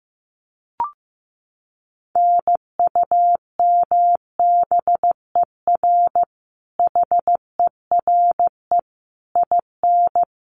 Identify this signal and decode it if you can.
Morse code, heard as a keyed tone, about 15 wpm, 700 Hz. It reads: NUMBER HERE IN
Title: Sets of 5 Words Encore2 15wpm